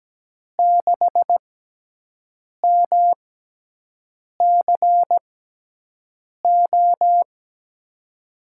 Morse code, heard as a keyed tone, about 17 wpm, spaced as for 6 wpm, 700 Hz. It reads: 6MCO
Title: USA Call Signs - 5 Characters - 6x Character Spacing - 17wpm